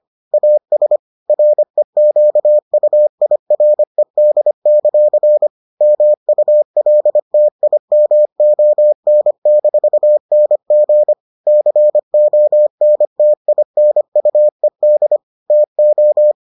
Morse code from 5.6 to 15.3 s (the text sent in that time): MULTIMON-NG CONTINUED